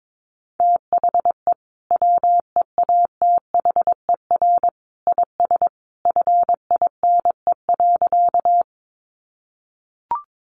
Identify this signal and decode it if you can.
Morse code, heard as a keyed tone, about 22 wpm, 700 Hz. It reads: THE WEATHER IS FINE.
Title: Sentences from Top 800 Words 22wpm